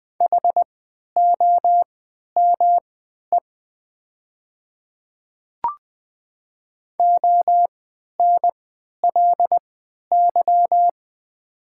Morse code, heard as a keyed tone, about 20 wpm, spaced as for 11 wpm, 700 Hz. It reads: HOME ONLY
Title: 3x Character Spacing - Top 300 Words - 20wpm